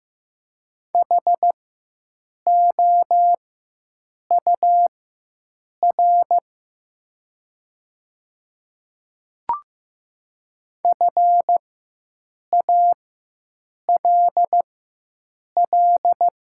Morse code, heard as a keyed tone, about 15 wpm, spaced as for 7 wpm, 700 Hz. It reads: HOUR FALL
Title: 4x Character Spacing - Top 500 Words - 15wpm